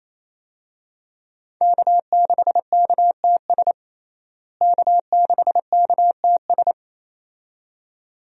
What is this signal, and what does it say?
Morse code, heard as a keyed tone, about 28 wpm, 700 Hz. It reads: K6KTS K6KTS
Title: USA Call Signs - 5 Characters - Repeated 2x - 28wpm